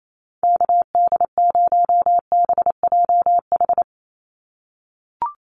Morse code, heard as a keyed tone, about 28 wpm, 700 Hz. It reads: KD0BJH
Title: USA Call Signs 28wpm